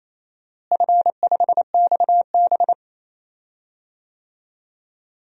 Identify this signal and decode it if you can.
Morse code, heard as a keyed tone, about 28 wpm, 700 Hz. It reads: F5XB
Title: DXCC-100 Call Signs - 4 Characters - Repeated 3x - 28wpm